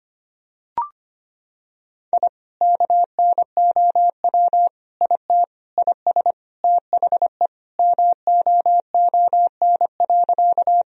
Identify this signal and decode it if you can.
Morse code, heard as a keyed tone, about 25 wpm, 700 Hz. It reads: I KNOW IT IS THE MOON.
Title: Sentences from Top 1000 Words - Review Part 3 - 25wpm